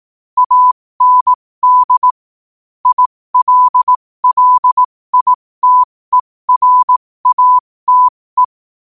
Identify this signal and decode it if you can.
Morse code, heard as a keyed tone, about 18 wpm, spaced as for 15 wpm, 1000 Hz. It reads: AND ILLITERATE
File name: qotdfiles / 15wpm